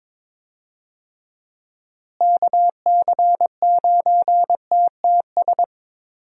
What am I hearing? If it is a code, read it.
Morse code, heard as a keyed tone, about 22 wpm, 700 Hz. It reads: KC9TTS
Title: USA Call Signs - 6 Characters - Repeated 3x - 22wpm